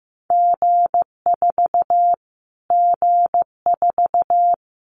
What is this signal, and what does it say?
Morse code, heard as a keyed tone, about 15 wpm, 700 Hz. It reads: G4 G4
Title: Callsign Prefixes for Top 25 DXCC - Repeated 3x - 15wpm